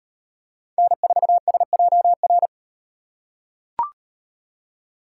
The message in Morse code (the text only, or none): N4SJR